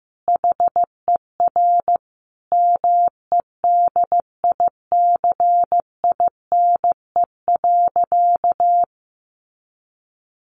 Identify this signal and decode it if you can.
Morse code, heard as a keyed tone, about 15 wpm, 700 Hz. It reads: HER MEDICINE.
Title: Sentences from Top 900 Words 15wpm